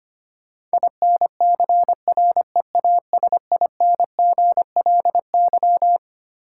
Morse code, heard as a keyed tone, about 25 wpm, 700 Hz. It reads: INCREASINGLY
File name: Twelve Letter Words - 25wpm